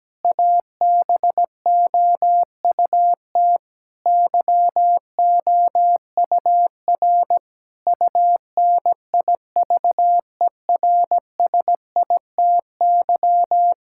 Morse code, heard as a keyed tone, about 17 wpm, 700 Hz. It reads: ABOUT YOUR UNIVERSITY
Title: Sentences from Top 1000 Words - Review Part 5 - 17wpm